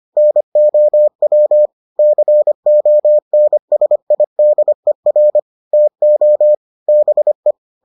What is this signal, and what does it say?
Morse code, heard as a keyed tone, about 25 wpm, 600 Hz. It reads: NOW CONSIDER TO BE